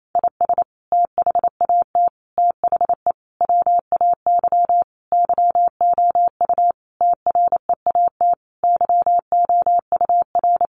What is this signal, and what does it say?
Morse code, heard as a keyed tone, about 28 wpm, 700 Hz. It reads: IS THAT THE WAY YOU TREAT YOUR
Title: Sentences from Top 1300 Words 28wpm